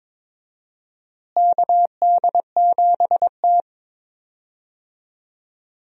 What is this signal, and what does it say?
Morse code, heard as a keyed tone, about 21 wpm, 700 Hz. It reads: KD7T
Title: USA Call Signs - 4 Characters - Repeated 2x - 22wpm